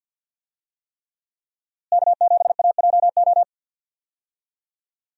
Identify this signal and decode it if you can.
Morse code, heard as a keyed tone, about 52 wpm, 700 Hz. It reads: K7AJO
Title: USA Call Signs - 5 Characters - Repeated 3x - 50wpm